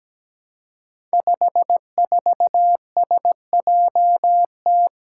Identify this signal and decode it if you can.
Morse code, heard as a keyed tone, about 17 wpm, 700 Hz. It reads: 54SJT
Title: Five Challenging Letter-Number Combinations 17wpm